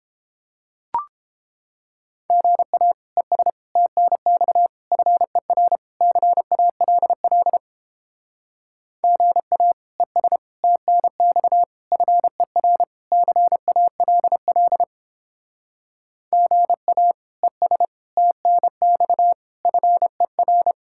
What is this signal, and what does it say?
Morse code, heard as a keyed tone, about 29 wpm, 700 Hz. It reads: GA ES TNX FER CALL GA ES TNX FER CALL GA ES TNX FER